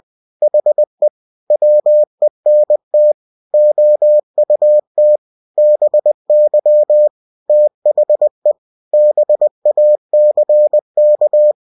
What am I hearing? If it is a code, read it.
Morse code, heard as a keyed tone, about 20 wpm, 600 Hz. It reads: HE WENT OUT BY THE BACK
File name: cw-books / winnie-the-pooh / winnie-20wpm-page-0016